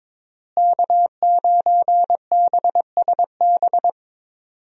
Morse code, heard as a keyed tone, about 22 wpm, 700 Hz. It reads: K9BSB
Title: USA Call Signs - 5 Characters - Repeated 3x - 22wpm